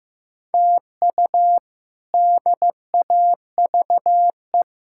Morse code, heard as a keyed tone, about 15 wpm, 700 Hz. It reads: TU DAVE
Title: K1USN SST QSO Practice - 15wpm